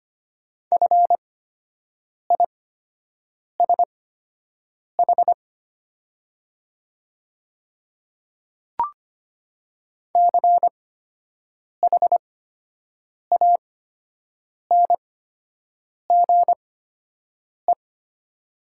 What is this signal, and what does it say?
Morse code, heard as a keyed tone, about 25 wpm, spaced as for 7 wpm, 700 Hz. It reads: FISH CHANGE
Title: 8x Character Spacing - Top 500 Words - 25wpm